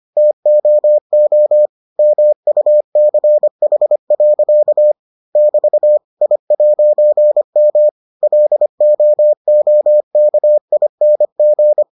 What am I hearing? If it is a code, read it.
Morse code, heard as a keyed tone, about 25 wpm, 600 Hz. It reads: TOO MUCH. <BT> I'M LOOKING